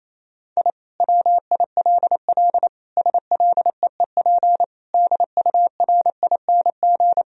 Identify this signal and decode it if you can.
Morse code, heard as a keyed tone, about 28 wpm, 700 Hz. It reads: I WILL SLEEP DURING
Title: Sentences from Top 1000 Words - Review Part 2 - 28wpm